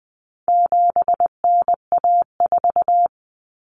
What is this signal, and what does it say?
Morse code, heard as a keyed tone, about 20 wpm, 700 Hz. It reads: 7NA4
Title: Common Cut Numbers - 4-Digit 20wpm